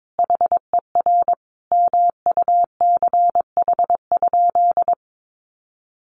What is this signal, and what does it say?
Morse code, heard as a keyed tone, about 22 wpm, 700 Hz. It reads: HER MUCH?
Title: Sentences from Top 400 Words 22wpm